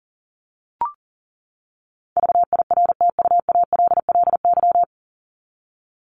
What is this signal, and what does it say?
Morse code, heard as a keyed tone, about 40 wpm, 700 Hz. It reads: VIRTUALLY
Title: LY Suffix - Top 100 Words with Base Words 40wpm